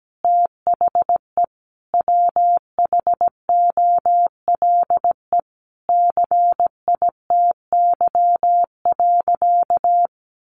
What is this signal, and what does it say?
Morse code, heard as a keyed tone, about 17 wpm, 700 Hz. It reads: THE WHOLE CITY.
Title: Sentences from Top 1400 Words 17wpm